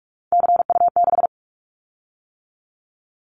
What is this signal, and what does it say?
Morse code, heard as a keyed tone, about 44 wpm, 700 Hz. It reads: CU6
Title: Callsign Prefixes for Top 100 DXCC - 45wpm